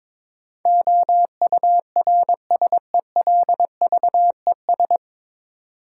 Morse code, heard as a keyed tone, about 23 wpm, 700 Hz. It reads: OURSELVES
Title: Sentences from Top 1600 Words - New Word Drill - 22wpm